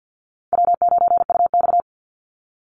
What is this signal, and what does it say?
Morse code, heard as a keyed tone, about 48 wpm, 700 Hz. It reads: W9UX
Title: NA SOTA Chaser Callsigns 50wpm